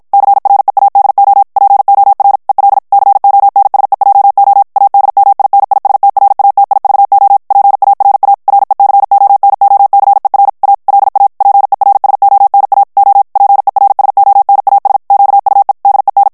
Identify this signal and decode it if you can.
Morse code, heard as a keyed tone, about 52 wpm, 800 Hz. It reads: CREADO POR EL CONSEJO ADMINISTRATIVO PARA RECONOCER A LA PERSONA O PERSONAS QUE DEN
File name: ESPAGNOL TEXTE EN CLAIR 1 VITESSE 50 MOTS MINUTES